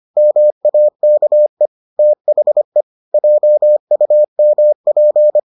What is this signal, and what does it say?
Morse code, heard as a keyed tone, about 25 wpm, 600 Hz. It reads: MAKE THE JUMP